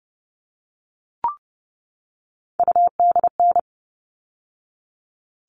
Morse code, heard as a keyed tone, about 30 wpm, 700 Hz. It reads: UDN
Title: All Cut Numbers - 3-Digit 30wpm